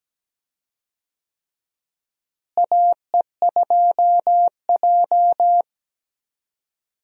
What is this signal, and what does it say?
Morse code, heard as a keyed tone, about 17 wpm, 700 Hz. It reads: AE2J
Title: USA Call Signs - 4 Characters - Repeated 3x - 17wpm